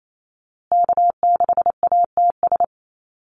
Morse code, heard as a keyed tone, about 28 wpm, 700 Hz. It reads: K6ATS
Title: USA Call Signs Repeated 2x 28wpm